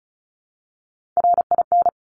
Morse code, heard as a keyed tone, about 36 wpm, 700 Hz. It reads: RIN
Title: Top Three Letter Combinations in Words 35wpm